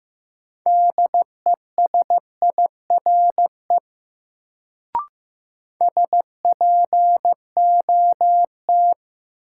Morse code, heard as a keyed tone, about 15 wpm, 700 Hz. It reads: DESIRE SPOT
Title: Sentences from Top 1100 Words - New Word Drill - 15wpm